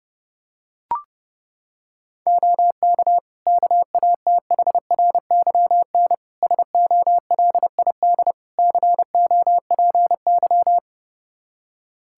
Morse code, heard as a keyed tone, about 32 wpm, 700 Hz. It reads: OK KATHRYN SOLID COPY
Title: K1USN SST QSO Practice 30wpm